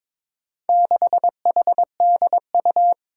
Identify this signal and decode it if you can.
Morse code, heard as a keyed tone, about 22 wpm, 700 Hz. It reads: 6HDU
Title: Four Challenging Letter-Number Combinations 22wpm